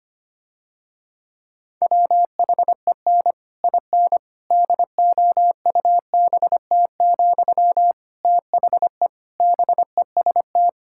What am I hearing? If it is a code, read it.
Morse code, heard as a keyed tone, about 25 wpm, 700 Hz. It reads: WHEN IN DOUBT, THE BEST
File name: practice / 25wpm / Difficult Sentences with Hints - 25wpm